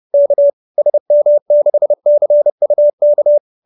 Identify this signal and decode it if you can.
Morse code, heard as a keyed tone, about 30 wpm, 590 Hz. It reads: K SM6CUK
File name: QSO319 30